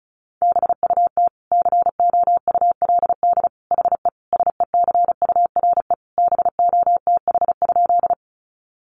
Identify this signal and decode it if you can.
Morse code, heard as a keyed tone, about 34 wpm, 700 Hz. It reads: BUT COULD HE SECURE BOTH?
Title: Sentences from Top 1200 Words 35wpm